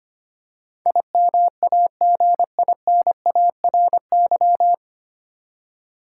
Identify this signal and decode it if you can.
Morse code, heard as a keyed tone, about 25 wpm, 700 Hz. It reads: IMAGINARY
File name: practice / 25wpm / 9 Letter Words - First 3 Letters Given - 25wpm